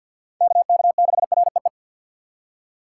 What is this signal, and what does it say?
Morse code, heard as a keyed tone, about 61 wpm, 700 Hz. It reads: KK6REE
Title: Speed Racing - USA 2x3 Call Signs 50wpm